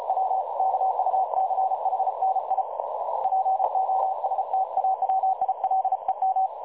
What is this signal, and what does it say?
Morse code, heard as a keyed tone, about 36 wpm, 760 Hz, with noise. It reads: W KUCHNI POSTAWILEM
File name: dlazaawansowanych / nagraniazpasma / websdr recording 3565.0kHz